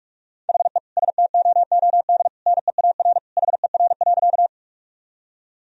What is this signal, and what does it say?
Morse code, heard as a keyed tone, about 46 wpm, 700 Hz. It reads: HE STOOD NEAR HER.
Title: Sentences from Top 1000 Words - Review Part 2 - 45wpm